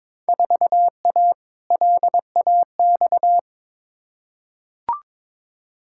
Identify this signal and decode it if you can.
Morse code, heard as a keyed tone, about 22 wpm, 700 Hz. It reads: 4A LAX
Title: Field Day Exchanges - 22wpm